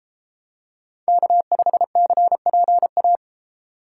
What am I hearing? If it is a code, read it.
Morse code, heard as a keyed tone, about 30 wpm, 700 Hz. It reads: K5CPA